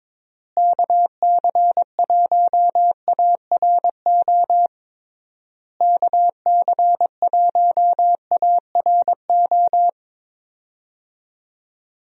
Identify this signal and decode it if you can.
Morse code, heard as a keyed tone, about 22 wpm, 700 Hz. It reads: KC1ARO KC1ARO
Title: USA Call Signs - 6 Characters - Repeated 3x - 22wpm